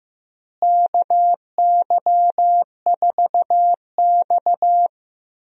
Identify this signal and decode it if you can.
Morse code, heard as a keyed tone, about 15 wpm, 700 Hz. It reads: KY4X
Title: USA Call Signs - 4 Characters - Repeated 3x - 15wpm